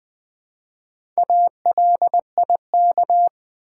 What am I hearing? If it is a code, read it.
Morse code, heard as a keyed tone, about 20 wpm, 700 Hz. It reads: ALIK
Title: Build A Word - 1 Letter At A Time - 5-Letter Words - 20wpm